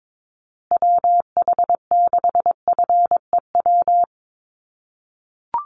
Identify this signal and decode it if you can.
Morse code, heard as a keyed tone, about 22 wpm, 700 Hz. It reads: WH6FEW
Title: USA Call Signs Repeated 3x 22wpm